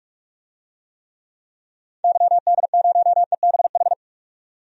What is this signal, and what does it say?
Morse code, heard as a keyed tone, about 46 wpm, 700 Hz. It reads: YD0EBH
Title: DXCC-100 Call Signs - 6 Characters - 45wpm